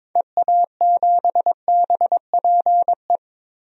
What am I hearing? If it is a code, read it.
Morse code, heard as a keyed tone, about 21 wpm, 700 Hz. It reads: EA7BPE
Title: DXCC-100 Call Signs - 6 Characters - 22wpm